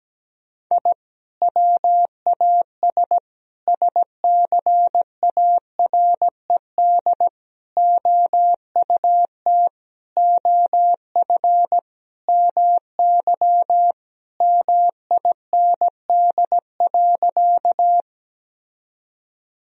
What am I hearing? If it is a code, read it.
Morse code, heard as a keyed tone, about 17 wpm, 700 Hz. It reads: I WAS SCARED OUT OF MY MIND.
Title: Sentences from Top 1000 Words - Review Part 4 - 17wpm